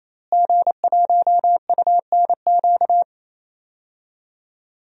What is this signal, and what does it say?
Morse code, heard as a keyed tone, about 29 wpm, 700 Hz. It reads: G1UNQ
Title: DXCC-100 Call Signs - 5 Characters - Repeated 3x - 28wpm